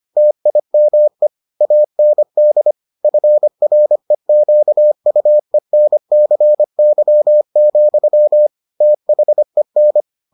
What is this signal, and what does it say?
Morse code, heard as a keyed tone, about 25 wpm, 600 Hz. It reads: TIME AND FREQUENCY, THEN